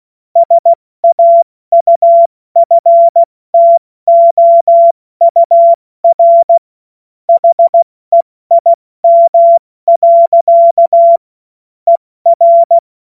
Morse code, heard as a keyed tone, about 16 wpm, 680 Hz. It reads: SAUFTOUR HEIM. ER
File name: Witz-16-September-2025-70-F80